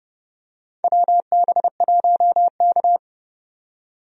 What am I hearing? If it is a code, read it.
Morse code, heard as a keyed tone, about 31 wpm, 700 Hz. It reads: WB1K
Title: USA Call Signs - 4 Characters - 30wpm